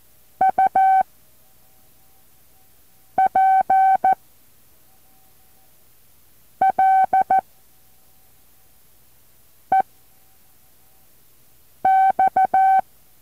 Morse code, heard as a keyed tone, about 14 wpm, 760 Hz. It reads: U P L E X
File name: intl morse code2